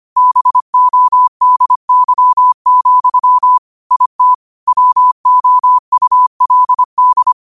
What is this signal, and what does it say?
Morse code, heard as a keyed tone, about 25 wpm, 1000 Hz. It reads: BODY, IT WOULD
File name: qotdfiles / 25wpm